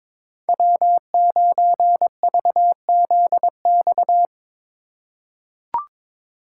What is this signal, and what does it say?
Morse code, heard as a keyed tone, about 22 wpm, 700 Hz. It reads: W9VZX
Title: USA Call Signs - 4x Character Spacing - 22wpm